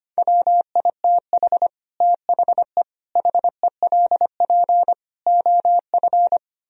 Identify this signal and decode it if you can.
Morse code, heard as a keyed tone, about 31 wpm, 700 Hz. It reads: WITH THE HELP OF
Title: Speed-Racing - Sets of 5 Words 25wpm